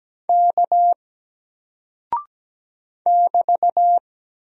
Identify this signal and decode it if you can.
Morse code, heard as a keyed tone, about 17 wpm, 700 Hz. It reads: K <BT>
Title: Single Character (Letter:Number:Punctuation:Prosign) Alt 17wpm